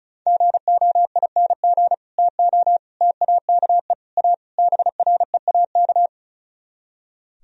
Morse code, heard as a keyed tone, about 42 wpm, 700 Hz. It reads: GOING TO TAKE A BREAK
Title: Speed-Racing - Sets of 5 Words 35wpm